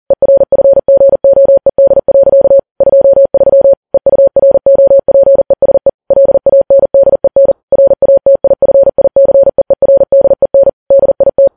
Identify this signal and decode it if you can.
Morse code, heard as a keyed tone, about 40 wpm, 550 Hz. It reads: ERFGOED. 23 EUROPESE LANDEN RATIFICEERDEN DIT
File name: Files Web tls / Morse on9fv / morse40kps